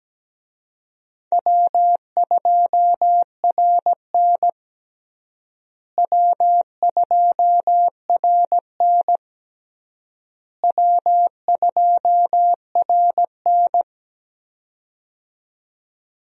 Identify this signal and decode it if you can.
Morse code, heard as a keyed tone, about 17 wpm, 700 Hz. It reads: W2RN W2RN W2RN
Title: USA Call Signs - 4 Characters - Repeated 3x - 17wpm